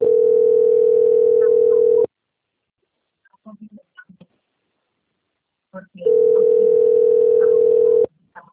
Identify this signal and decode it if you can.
Ringback tone (North America), 440 + 480 Hz.